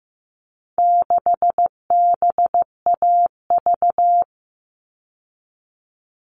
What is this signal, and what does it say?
Morse code, heard as a keyed tone, about 15 wpm, 700 Hz. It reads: 6BAV
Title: All Cut Numbers - 4-Digit 15wpm